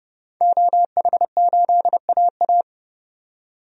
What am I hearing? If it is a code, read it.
Morse code, heard as a keyed tone, about 30 wpm, 700 Hz. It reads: OH8AA
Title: DXCC-100 Call Signs - 5 Characters - 30wpm